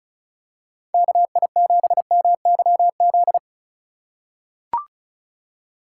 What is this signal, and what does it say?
Morse code, heard as a keyed tone, about 36 wpm, 700 Hz. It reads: KI7MYZ
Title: USA Call Signs - 6 Characters - 3x Character Spacing - 35wpm